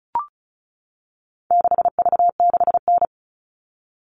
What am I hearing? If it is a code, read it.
Morse code, heard as a keyed tone, about 34 wpm, 700 Hz. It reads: 6V6N
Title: All Cut Numbers - 4-Digit 35wpm